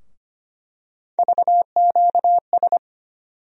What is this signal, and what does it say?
Morse code, heard as a keyed tone, about 25 wpm, 700 Hz. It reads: VQS
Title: USA Call Sign Postfixes - 3 Letters 25wpm